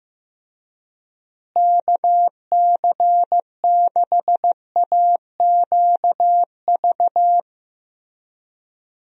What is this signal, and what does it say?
Morse code, heard as a keyed tone, about 15 wpm, 700 Hz. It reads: KC6AQV
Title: USA Call Signs - 6 Characters - Repeated 2x - 15wpm